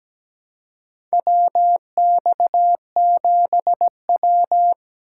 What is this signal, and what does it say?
Morse code, heard as a keyed tone, about 17 wpm, 700 Hz. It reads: WX7W
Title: USA Call Signs - 4 Characters - 3x Character Spacing - 17wpm